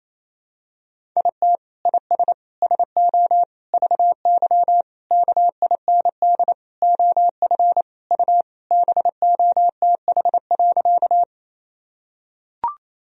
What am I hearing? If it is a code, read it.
Morse code, heard as a keyed tone, about 28 wpm, 700 Hz. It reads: IT IS SO VY KIND OF U BOTH.
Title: Sentences from Top 300 Words and Abbreviations 28wpm